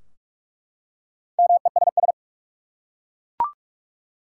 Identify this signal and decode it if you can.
Morse code, heard as a keyed tone, about 46 wpm, 700 Hz. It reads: MESS
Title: Four Letter Words - 45wpm